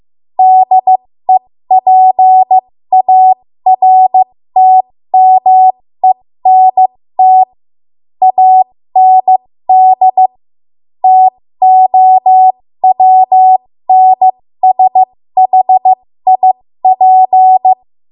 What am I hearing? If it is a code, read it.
Morse code, heard as a keyed tone, about 15 wpm, spaced as for 13 wpm, 750 Hz. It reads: DEPARTMENT AND TOWNSHIP
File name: Morse / 13WPM